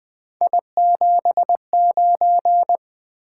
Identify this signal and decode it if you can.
Morse code, heard as a keyed tone, about 20 wpm, 700 Hz. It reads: I79
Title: Three Letter-Number Combinations - 20wpm